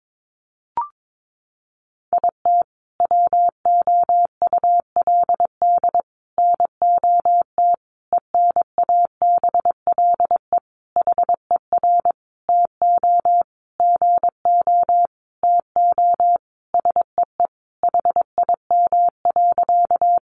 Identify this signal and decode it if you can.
Morse code, heard as a keyed tone, about 22 wpm, 700 Hz. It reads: IT WOULD NOT ENABLE HER TO GO TO SEE HIM.
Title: Sentences from Top 1500 Words 22wpm